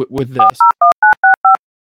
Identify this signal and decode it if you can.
DTMF, heard as telephone touch tones, tones of about 130 ms, with speech. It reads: 401965